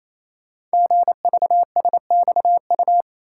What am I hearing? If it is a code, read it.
Morse code, heard as a keyed tone, about 27 wpm, 700 Hz. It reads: GVSXU
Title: Code Groups - 5 Letters - 28wpm